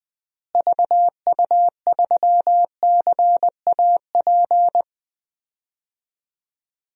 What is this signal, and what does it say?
Morse code, heard as a keyed tone, about 20 wpm, 700 Hz. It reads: VU3CAP
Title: DXCC-100 Call Signs - 6 Characters - Repeated 3x - 20wpm